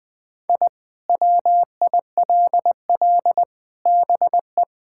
Morse code, heard as a keyed tone, about 20 wpm, 700 Hz. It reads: I WILL BE
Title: Sentences from Top 1000 Words - Review Part 6 - 20wpm